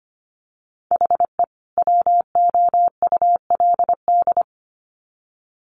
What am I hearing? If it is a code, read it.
Morse code, heard as a keyed tone, about 25 wpm, 700 Hz. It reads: HE WOULD
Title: Sets of 2 Words 25wpm